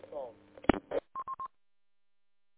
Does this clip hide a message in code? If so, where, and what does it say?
No, only speech.